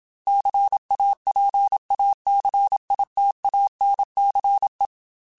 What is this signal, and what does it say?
Morse code, heard as a keyed tone, about 27 wpm, 800 Hz. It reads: CAPACITANCE